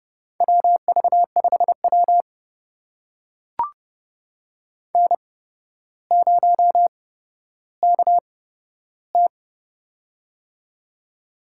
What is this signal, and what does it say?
Morse code, heard as a keyed tone, about 31 wpm, spaced as for 13 wpm, 700 Hz. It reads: WV5W N0KT
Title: USA Call Signs - 4 Characters - 8x Character Spacing - 30wpm